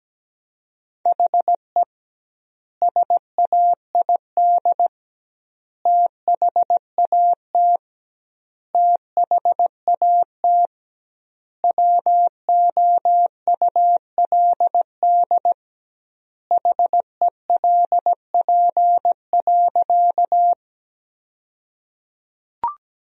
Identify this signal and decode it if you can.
Morse code, heard as a keyed tone, about 17 wpm, 700 Hz. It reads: HE SAID THAT THAT WOULD HELP.
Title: 2x Word-Spacing - Sentences from Top 200 Words - 17wpm